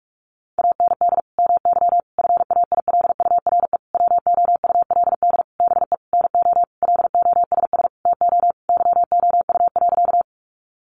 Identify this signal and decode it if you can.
Morse code, heard as a keyed tone, about 44 wpm, 700 Hz. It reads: AND MY FAILURE WOULD BE NO LOSS TO YOU.
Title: Sentences from Top 1400 Words 45wpm